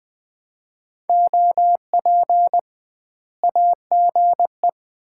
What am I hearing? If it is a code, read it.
Morse code, heard as a keyed tone, about 21 wpm, 700 Hz. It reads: OP AGE
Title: Common QSO Phrases - 2x Word-Spacing - 20wpm